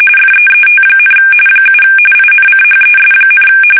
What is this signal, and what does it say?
RTTY, about 45.45 baud. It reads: FEET, 7 INCHES,
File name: fsk850 / battleshipnj